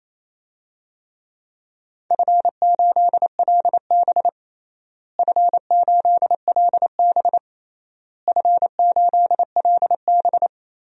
Morse code, heard as a keyed tone, about 28 wpm, 700 Hz. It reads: F8LB F8LB F8LB
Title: DXCC-100 Call Signs - 4 Characters - Repeated 3x - 28wpm